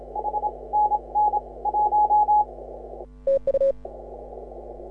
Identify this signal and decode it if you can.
Morse code, heard as a keyed tone, about 27 wpm, spaced as for 23 wpm, 840 Hz. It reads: H NN 1
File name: AUDIO2018 / HA6OI CW